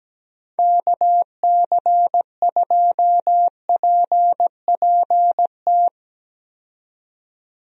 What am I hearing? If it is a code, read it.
Morse code, heard as a keyed tone, about 17 wpm, 700 Hz. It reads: KC2PPT
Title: USA Call Signs - 6 Characters - 17wpm